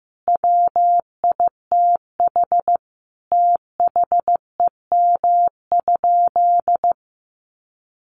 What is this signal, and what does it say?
Morse code, heard as a keyed tone, about 15 wpm, 700 Hz. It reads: WITH THEM?
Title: Sentences from Top 800 Words 15wpm